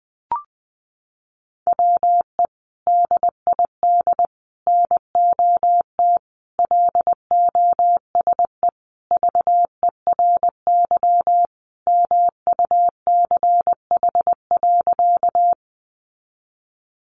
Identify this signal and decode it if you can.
Morse code, heard as a keyed tone, about 20 wpm, 700 Hz. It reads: WE DID NOT LOSE VERY MUCH.
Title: Sentences from Top 1100 Words 20wpm